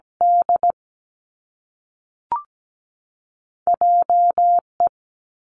Morse code, heard as a keyed tone, about 17 wpm, 700 Hz. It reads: D J E
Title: Two Challenging Letter-Number Combinations 17wpm